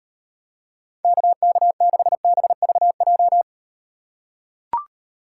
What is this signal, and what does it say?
Morse code, heard as a keyed tone, about 38 wpm, 700 Hz. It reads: KK6BVJ